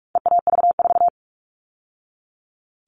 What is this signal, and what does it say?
Morse code, heard as a keyed tone, about 44 wpm, 700 Hz. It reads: EAV4
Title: All Cut Numbers - 4-Digit 45wpm